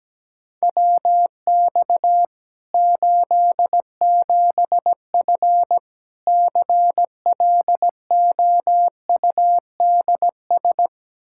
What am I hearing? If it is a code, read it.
Morse code, heard as a keyed tone, about 17 wpm, 700 Hz. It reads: WX 87F CLOUDS
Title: K1USN SST QSO Practice 17wpm